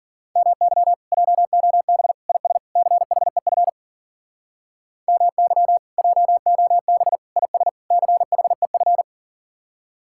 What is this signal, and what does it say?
Morse code, heard as a keyed tone, about 47 wpm, 700 Hz. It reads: MY JOB IS CHEF MY JOB IS CHEF
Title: Speed-Racing Common QSO Phrases Encore - 40wpm